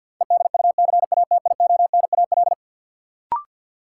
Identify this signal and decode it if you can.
Morse code, heard as a keyed tone, about 53 wpm, 700 Hz. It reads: EDUCATIONAL
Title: Sentences from Top 1200 Words - New Word Drill Alt - 50wpm